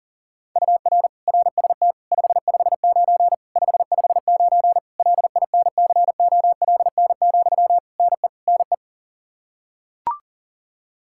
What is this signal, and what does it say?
Morse code, heard as a keyed tone, about 37 wpm, 700 Hz. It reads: UR RST 559 559 LINCOLN, NE NE
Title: Practice QSO - Exchange 1 - Beginning - 40wpm